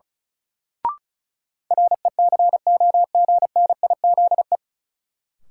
Morse code, heard as a keyed tone, about 36 wpm, 700 Hz. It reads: RECOGNIZE
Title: Sentences from Top 1400 Words - New Word Drill Alt - 35wpm